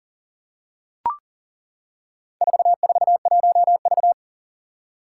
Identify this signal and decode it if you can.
Morse code, heard as a keyed tone, about 41 wpm, 700 Hz. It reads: 441V
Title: Four Challenging Letter-Number Combinations 40wpm